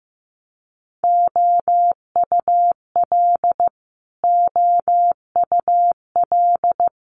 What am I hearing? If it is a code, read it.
Morse code, heard as a keyed tone, about 15 wpm, 700 Hz. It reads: OUL OUL
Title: Multi-letter Phonemes Repeated 3x 15wpm